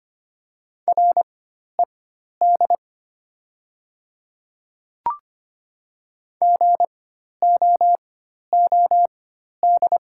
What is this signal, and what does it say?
Morse code, heard as a keyed tone, about 25 wpm, spaced as for 12 wpm, 690 Hz. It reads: RED GOOD
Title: 4x Character Spacing - Top 500 Words - 25wpm